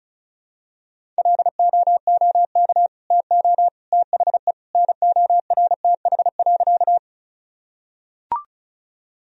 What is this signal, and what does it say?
Morse code, heard as a keyed tone, about 36 wpm, 700 Hz. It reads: LOOK TO THE NORTH.
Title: Sentences from Top 1000 Words - Review Part 1 - 35wpm